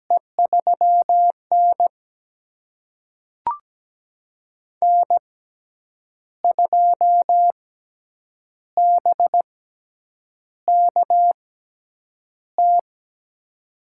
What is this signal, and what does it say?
Morse code, heard as a keyed tone, about 17 wpm, spaced as for 7 wpm, 700 Hz. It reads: E3N N2BKT
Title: USA Call Signs - 6x Character Spacing - 17wpm